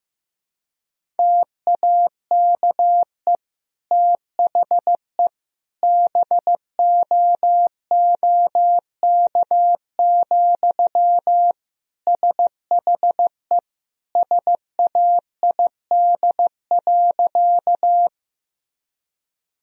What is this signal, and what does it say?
Morse code, heard as a keyed tone, about 16 wpm, 700 Hz. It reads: TAKE THE BOOK, SHE SAID.
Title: Sentences from Top 1000 Words - Review Part 1 - 15wpm